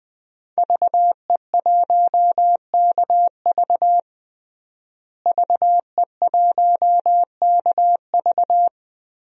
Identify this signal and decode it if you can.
Morse code, heard as a keyed tone, about 20 wpm, 700 Hz. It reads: VE1KV VE1KV
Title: DXCC-100 Call Signs - 5 Characters - Repeated 2x - 20wpm